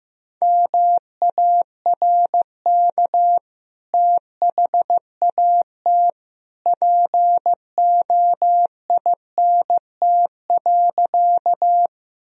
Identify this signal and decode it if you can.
Morse code, heard as a keyed tone, about 15 wpm, 700 Hz. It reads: MARK THAT POINT.
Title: Sentences from Top 1000 Words - Review Part 1 - 15wpm